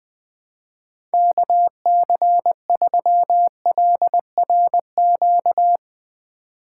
Morse code, heard as a keyed tone, about 20 wpm, 700 Hz. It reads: KC3LRQ